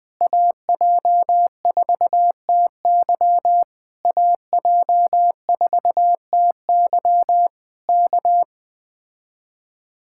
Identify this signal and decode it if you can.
Morse code, heard as a keyed tone, about 20 wpm, 700 Hz. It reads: AJ4TY AJ4TY K
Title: Speed-Racing Common QSO Phrases - 17wpm